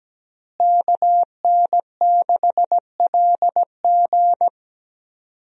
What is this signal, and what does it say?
Morse code, heard as a keyed tone, about 17 wpm, 700 Hz. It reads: KN6LG
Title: USA Call Signs - 5 Characters - Repeated 2x - 17wpm